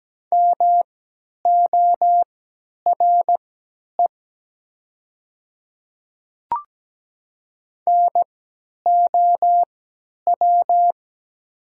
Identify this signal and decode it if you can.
Morse code, heard as a keyed tone, about 17 wpm, spaced as for 10 wpm, 700 Hz. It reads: MORE NOW
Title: 3x Character Spacing - Top 500 Words - 17wpm